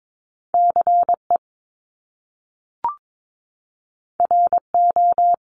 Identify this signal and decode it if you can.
Morse code, heard as a keyed tone, about 22 wpm, 700 Hz. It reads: CE RO
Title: Top Two Letter Combinations in Words 22wpm